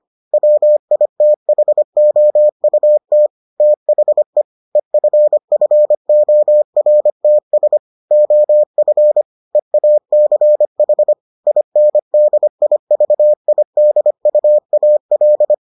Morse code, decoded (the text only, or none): WITHOUT THE EFFORTS OF EACH INDIVIDUAL